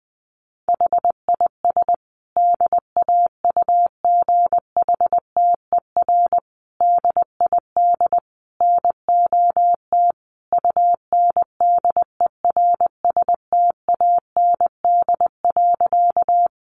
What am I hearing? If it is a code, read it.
Morse code, heard as a keyed tone, about 20 wpm, 700 Hz. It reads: HIS DAUGHTER DID NOT UNDERSTAND.
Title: Sentences from Top 1500 Words 20wpm